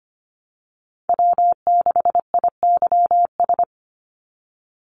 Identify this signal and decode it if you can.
Morse code, heard as a keyed tone, about 25 wpm, 700 Hz. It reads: W6IYS
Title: Call Signs Repeated 3x - 25wpm